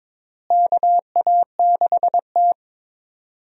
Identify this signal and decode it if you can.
Morse code, heard as a keyed tone, about 23 wpm, 700 Hz. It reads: KA6T
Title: USA Call Signs - 4 Characters - Repeated 2x - 22wpm